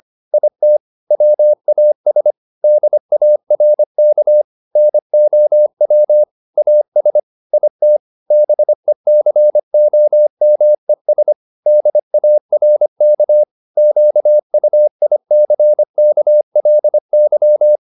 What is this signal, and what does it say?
Morse code, heard as a keyed tone, about 25 wpm, 600 Hz. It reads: IT WAS DARK NOW AS IT BECOMES DARK QUICKLY